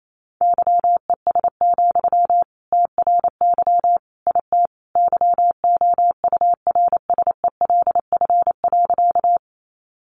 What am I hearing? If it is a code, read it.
Morse code, heard as a keyed tone, about 28 wpm, 700 Hz. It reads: YES, TRY IT YOURSELF.
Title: Sentences from Top 600 Words 28wpm